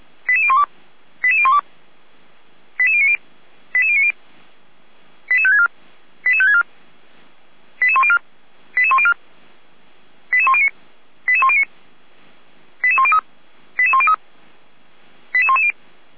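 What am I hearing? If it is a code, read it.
Five-tone selective call (ZVEI), heard as quick runs of five tones, tones of about 70 ms. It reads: 88012 88012 88090 88090 88564 88564 88104 88104 88109 88109 88103 88103 88110
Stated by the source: ZVEI five-tone selective call 88219, received with an SDR.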